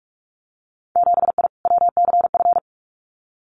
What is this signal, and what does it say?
Morse code, heard as a keyed tone, about 46 wpm, 700 Hz. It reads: 7I WCF
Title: Winter Field Day Exchanges Encore 45wpm